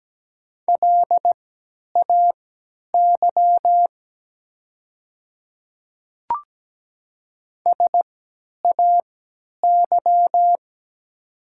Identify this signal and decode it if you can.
Morse code, heard as a keyed tone, about 17 wpm, spaced as for 10 wpm, 700 Hz. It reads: LAY SAY
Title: 3x Character Spacing - Top 500 Words - 17wpm